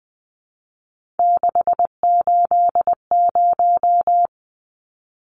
Morse code, heard as a keyed tone, about 20 wpm, 700 Hz. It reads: 680
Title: Numbers - 3-Digit 20wpm